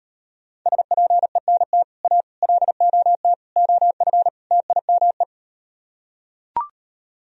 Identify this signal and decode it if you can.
Morse code, heard as a keyed tone, about 39 wpm, 700 Hz. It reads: SPENT A LOT OF TIME